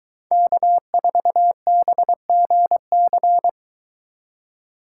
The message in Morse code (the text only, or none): K4BGC